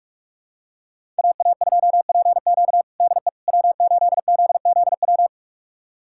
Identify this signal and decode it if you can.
Morse code, heard as a keyed tone, about 41 wpm, 700 Hz. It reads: AA2PQ DE W8ZZW
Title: USA Callsign DE Callsign 45wpm